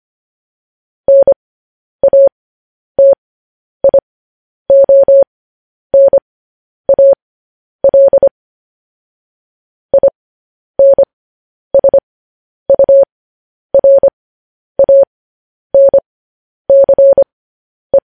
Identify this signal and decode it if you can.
Morse code, heard as a keyed tone, about 25 wpm, spaced as for 10 wpm, 550 Hz. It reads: NATIONAL INSURANCE
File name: podcast / News headlines-1x-25x10